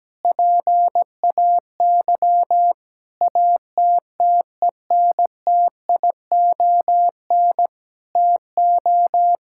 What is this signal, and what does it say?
Morse code, heard as a keyed tone, about 17 wpm, 700 Hz. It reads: PAY ATTENTION TO
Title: Sentences from Top 1000 Words - Review Part 6 - 17wpm